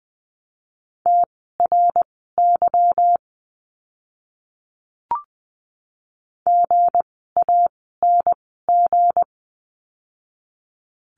Morse code, heard as a keyed tone, about 20 wpm, spaced as for 15 wpm, 700 Hz. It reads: TRY GANG
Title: 160 Lesson 26f - Letter g - Words 20wpm